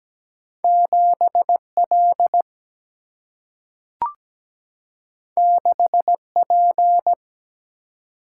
Morse code, heard as a keyed tone, about 17 wpm, 700 Hz. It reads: 7L 6P
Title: Single-Letter Single-Number 17wpm